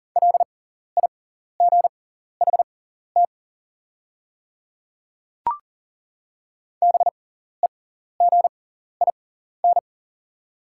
Morse code, heard as a keyed tone, about 40 wpm, spaced as for 14 wpm, 690 Hz. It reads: LIGHT BEGIN
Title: 6x Character Spacing - Top 300 Words - 40wpm